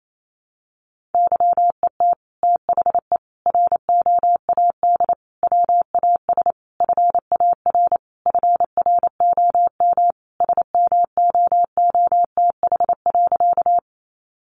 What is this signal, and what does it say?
Morse code, heard as a keyed tone, about 28 wpm, 700 Hz. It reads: YET THE ROAD WAS FAR FROM SMOOTH.
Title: Sentences from Top 1400 Words 28wpm